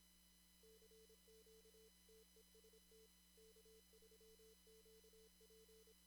Morse code, heard as a keyed tone, about 25 wpm, 440 Hz. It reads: CQ TEST K3QP
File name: CW / kt5j wpx cw-op-k5pi